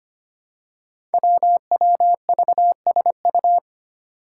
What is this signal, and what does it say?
Morse code, heard as a keyed tone, about 25 wpm, 700 Hz. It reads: WWVSU
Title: Five Challenging Letter-Number Combinations 25wpm